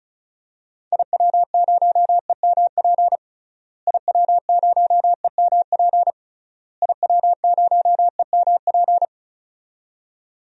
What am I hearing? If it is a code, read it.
Morse code, heard as a keyed tone, about 36 wpm, 700 Hz. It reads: IW0EMP IW0EMP IW0EMP
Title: DXCC-100 Call Signs - 6 Characters - Repeated 3x - 35wpm